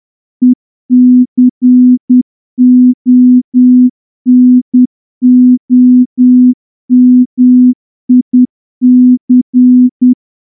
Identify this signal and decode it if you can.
Morse code, heard as a keyed tone, about 9 wpm, 250 Hz. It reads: ECONOMIC
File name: morse-training-10WPM